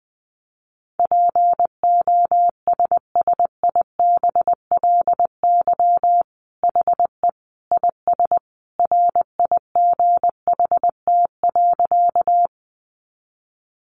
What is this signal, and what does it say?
Morse code, heard as a keyed tone, about 20 wpm, 700 Hz. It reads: POSSIBLY HE IS RIGHT.
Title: Sentences from Top 1300 Words 20wpm